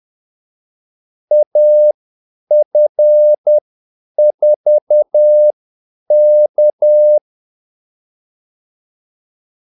Morse code, heard as a keyed tone, about 10 wpm, spaced as for 8 wpm, 600 Hz. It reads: AF4K
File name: Callsigns-10-8wpm